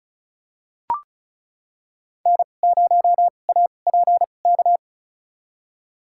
Morse code, heard as a keyed tone, about 36 wpm, spaced as for 25 wpm, 700 Hz. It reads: N0APK
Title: USA Call Signs - 5 Characters - 2x Character Spacing - 35wpm